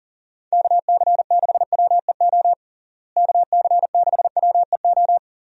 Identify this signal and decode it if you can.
Morse code, heard as a keyed tone, about 40 wpm, 700 Hz. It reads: KC6WEO KC6WEO
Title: USA Call Signs - 6 Characters - Repeated 2x - 40wpm